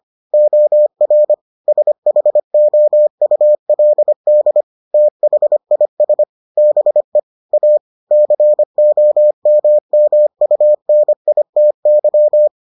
Morse code, heard as a keyed tone, about 25 wpm, 600 Hz. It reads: OR SHOULD THIS BE A COMMUNITY